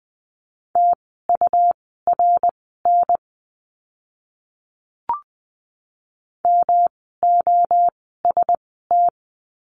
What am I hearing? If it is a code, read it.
Morse code, heard as a keyed tone, about 20 wpm, spaced as for 14 wpm, 700 Hz. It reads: TURN MOST
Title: 108 Lesson 18f - Letter m - Words 20wpm